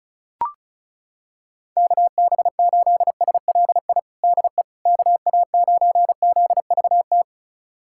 Morse code, heard as a keyed tone, about 36 wpm, 700 Hz. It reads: KB8SLI DE KA9ZVT
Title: USA Callsign DE Callsign - 6 Characters - 35wpm